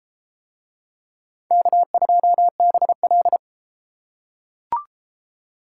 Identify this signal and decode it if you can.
Morse code, heard as a keyed tone, about 33 wpm, 700 Hz. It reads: K2BL